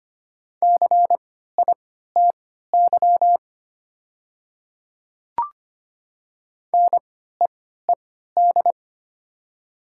Morse code, heard as a keyed tone, about 25 wpm, spaced as for 14 wpm, 700 Hz. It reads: CITY NEED
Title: 3x Character Spacing - Top 300 Words - 25wpm